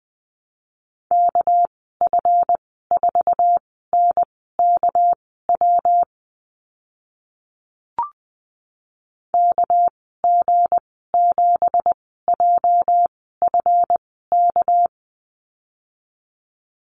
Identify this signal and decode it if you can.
Morse code, heard as a keyed tone, about 20 wpm, spaced as for 14 wpm, 700 Hz. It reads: KF4NKW KG7JFK
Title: 238 Lesson 38g - Letter q - Call Signs 20wpm